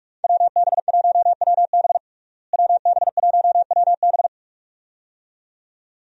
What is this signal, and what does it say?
Morse code, heard as a keyed tone, about 47 wpm, 700 Hz. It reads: WB1WB WB1WB
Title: USA Call Signs - 5 Characters - Repeated 2x - 45wpm